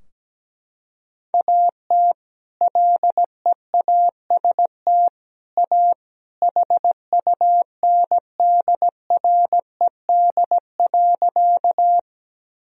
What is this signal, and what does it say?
Morse code, heard as a keyed tone, about 17 wpm, 700 Hz. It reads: AT LEAST A HUNDRED.
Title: Sentences from Top 1000 Words - Review Part 2 - 17wpm